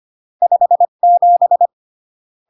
Morse code, heard as a keyed tone, about 25 wpm, 700 Hz. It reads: 57
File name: CW / jodorowsky / 82 commandments 25-20